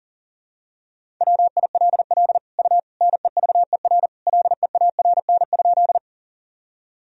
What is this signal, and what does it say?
Morse code, heard as a keyed tone, about 40 wpm, 700 Hz. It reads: WILL U NEVER LEARN?
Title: Sentences from Top 300 Words and Abbreviations 40wpm